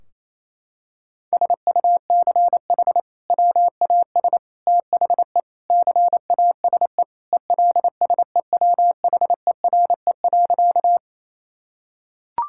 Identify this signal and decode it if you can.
Morse code, heard as a keyed tone, about 28 wpm, 700 Hz. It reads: SUCH WAS THE CASE ELSEWHERE.
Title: Sentences from Top 1600 Words 28wpm